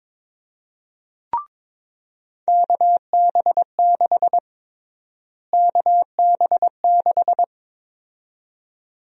Morse code, heard as a keyed tone, about 22 wpm, 700 Hz. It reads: KB6 KB6
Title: USA Call Sign Prefixes - Repeated 2x - 22wpm